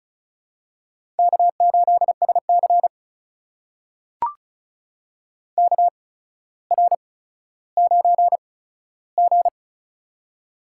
Morse code, heard as a keyed tone, about 34 wpm, spaced as for 15 wpm, 700 Hz. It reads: K8SC KR9G
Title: USA Call Signs - 4 Characters - 8x Character Spacing - 35wpm